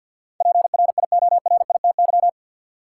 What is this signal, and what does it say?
Morse code, heard as a keyed tone, about 52 wpm, 700 Hz. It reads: PRIORITY
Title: Sentences from Top 1600 Words - New Word Drill - 50wpm